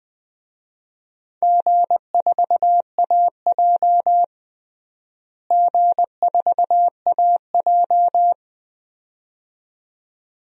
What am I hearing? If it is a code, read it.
Morse code, heard as a keyed tone, about 20 wpm, 700 Hz. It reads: G4AJ G4AJ
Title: DXCC-100 Call Signs - 4 Characters - Repeated 2x - 20wpm